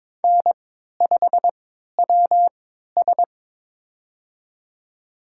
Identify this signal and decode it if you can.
Morse code, heard as a keyed tone, about 22 wpm, spaced as for 13 wpm, 700 Hz. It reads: N5WS
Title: USA Call Signs - 4 Characters - 3x Character Spacing - 22wpm